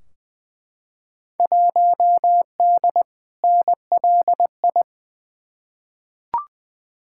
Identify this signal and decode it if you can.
Morse code, heard as a keyed tone, about 21 wpm, 700 Hz. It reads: 1D NLI
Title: Field Day Exchanges - 20wpm